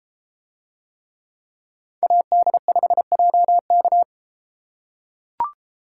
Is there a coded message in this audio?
Yes: Morse code from 2.0 to 4.0 s, heard as a keyed tone, about 32 wpm, 700 Hz. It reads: AD5JK